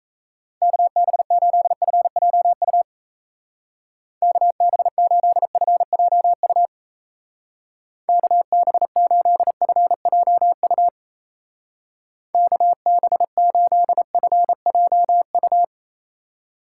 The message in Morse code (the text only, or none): KB8FJU KB8FJU KB8FJU KB8FJU